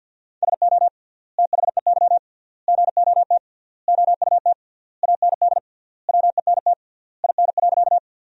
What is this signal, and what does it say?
Morse code, heard as a keyed tone, about 50 wpm, 700 Hz. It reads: SO THEY GOT OUT AND WENT IN.
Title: 3x Word-Spacing - Sentences from Top 300 Words - 50wpm